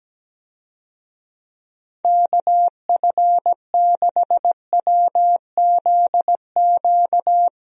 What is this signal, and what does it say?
Morse code, heard as a keyed tone, about 17 wpm, 700 Hz. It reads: KF6WZQ
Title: USA Call Signs - 6 Characters - Repeated 3x - 17wpm